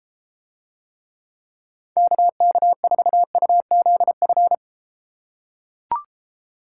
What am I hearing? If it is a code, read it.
Morse code, heard as a keyed tone, about 33 wpm, 700 Hz. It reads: KK4UZF